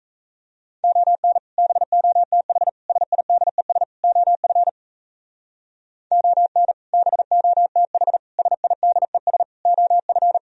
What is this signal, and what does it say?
Morse code, heard as a keyed tone, about 34 wpm, 700 Hz. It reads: ON BOTH SIDES OF ON BOTH SIDES OF
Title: Speed-Racing - Sets of 4 Words 28wpm